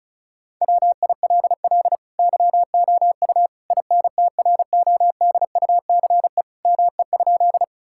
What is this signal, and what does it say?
Morse code, heard as a keyed tone, about 34 wpm, 700 Hz. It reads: WILL YOU INTRODUCE ME?
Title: Sentences from Top 1600 Words 35wpm